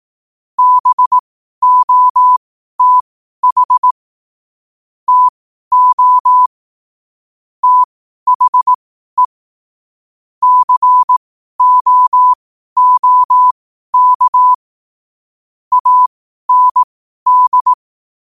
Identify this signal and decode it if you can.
Morse code, heard as a keyed tone, about 18 wpm, spaced as for 12 wpm, 1000 Hz. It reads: BOTH TO THE COOK AND